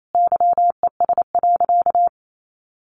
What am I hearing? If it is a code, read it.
Morse code, heard as a keyed tone, about 28 wpm, 700 Hz. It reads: YES.
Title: Sentences from Top 500 Words 28wpm